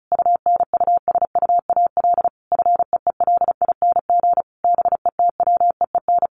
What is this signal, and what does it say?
Morse code, heard as a keyed tone, about 34 wpm, 700 Hz. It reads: UNUSUAL FEELING BETWEEN
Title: Sentences from Top 1400 Words 35wpm